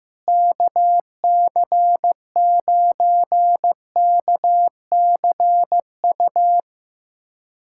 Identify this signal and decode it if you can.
Morse code, heard as a keyed tone, about 15 wpm, 700 Hz. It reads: KC9KCU
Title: USA Callsign DE Callsign 15wpm